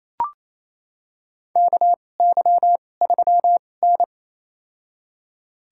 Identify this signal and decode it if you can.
Morse code, heard as a keyed tone, about 28 wpm, spaced as for 20 wpm, 700 Hz. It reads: KY3N
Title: USA Call Signs - 4 Characters - 2x Character Spacing - 28wpm